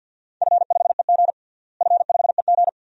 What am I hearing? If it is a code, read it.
Morse code, heard as a keyed tone, about 46 wpm, 700 Hz. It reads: F5EG F5EG
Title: DXCC-100 Call Signs - 4 Characters - Repeated 2x - 50wpm